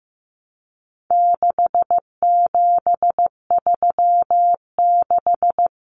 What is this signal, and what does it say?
Morse code, heard as a keyed tone, about 15 wpm, 700 Hz. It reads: 6736
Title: Common Cut Numbers - 4-Digit 15wpm